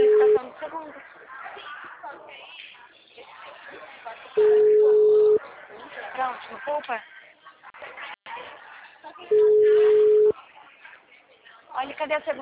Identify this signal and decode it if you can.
Ringback tone (Europe, 425 Hz), with speech.